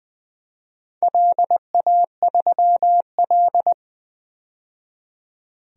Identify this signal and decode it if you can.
Morse code, heard as a keyed tone, about 21 wpm, 700 Hz. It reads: LA3L
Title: DXCC-100 Call Signs - 4 Characters - 20wpm